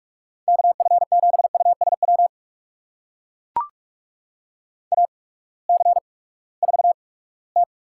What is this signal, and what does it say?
Morse code, heard as a keyed tone, about 42 wpm, spaced as for 22 wpm, 700 Hz. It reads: KF7USW AC4T
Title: USA Call Signs - 8x Character Spacing - 45wpm